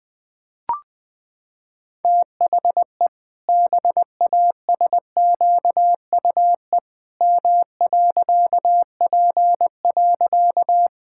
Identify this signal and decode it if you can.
Morse code, heard as a keyed tone, about 20 wpm, 700 Hz. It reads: THE BASQUE M.P.
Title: Pangram Challenge v2 - 20wpm